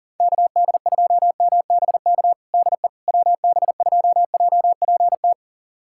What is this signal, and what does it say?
Morse code, heard as a keyed tone, about 40 wpm, 700 Hz. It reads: KD2MBK DE WB2JPT
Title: USA Callsign DE Callsign 40wpm